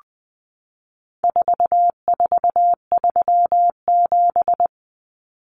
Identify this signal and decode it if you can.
Morse code, heard as a keyed tone, about 20 wpm, 700 Hz. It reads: 4437
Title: Numbers - 4-Digit 20wpm